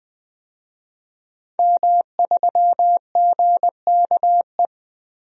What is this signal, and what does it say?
Morse code, heard as a keyed tone, about 20 wpm, 700 Hz. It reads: M3GKE
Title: DXCC-100 Call Signs - 5 Characters - 20wpm